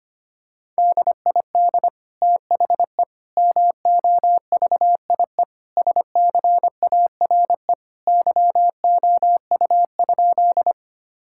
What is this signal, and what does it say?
Morse code, heard as a keyed tone, about 25 wpm, 700 Hz. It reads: DID THE MOVIE SCARE YOU?
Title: Sentences from Top 1000 Words - Review Part 5 - 25wpm